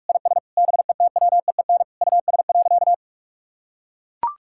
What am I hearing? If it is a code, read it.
Morse code, heard as a keyed tone, about 42 wpm, 690 Hz. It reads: IS BETWEEN US.
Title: Sentences from Top 1000 Words - Review Part 6 - 45wpm